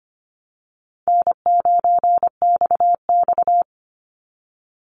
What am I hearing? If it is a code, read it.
Morse code, heard as a keyed tone, about 25 wpm, 700 Hz. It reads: N9XX
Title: Call Signs Repeated 2x - 25wpm